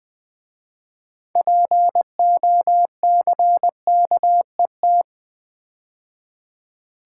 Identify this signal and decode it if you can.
Morse code, heard as a keyed tone, about 20 wpm, 700 Hz. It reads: POCKET
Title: 6 Letter Words - First 3 Letters Given - 20wpm